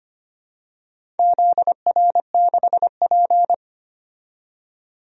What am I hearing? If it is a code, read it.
Morse code, heard as a keyed tone, about 25 wpm, 700 Hz. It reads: ZR6P
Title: DXCC-100 Call Signs - 4 Characters - 25wpm